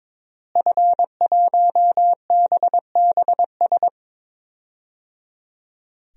Morse code, heard as a keyed tone, about 22 wpm, 700 Hz. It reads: F1BBS
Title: DXCC-100 Call Signs - 5 Characters - Repeated 3x - 22wpm